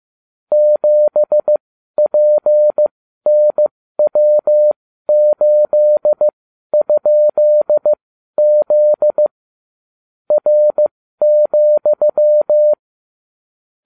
Morse code, heard as a keyed tone, about 15 wpm, spaced as for 12 wpm, 600 Hz. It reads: 7PNW8?Z R,